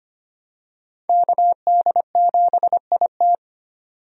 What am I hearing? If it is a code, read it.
Morse code, heard as a keyed tone, about 25 wpm, 700 Hz. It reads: KD7IT
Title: USA Call Signs - 5 Characters - 25wpm